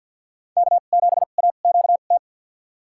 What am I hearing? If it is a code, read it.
Morse code, heard as a keyed tone, about 53 wpm, spaced as for 37 wpm, 700 Hz. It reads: K7AQT
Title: USA Call Signs - 5 Characters - 2x Character Spacing - 50wpm